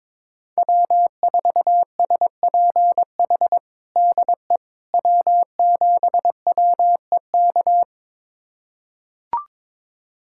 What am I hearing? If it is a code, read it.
Morse code, heard as a keyed tone, about 22 wpm, 700 Hz. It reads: W4SPH DE W7WEK
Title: USA Callsign DE Callsign - 5 Characters - 22wpm